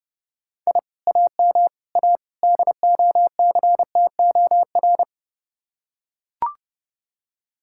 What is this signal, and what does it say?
Morse code, heard as a keyed tone, about 29 wpm, 700 Hz. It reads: I AM A DOCTOR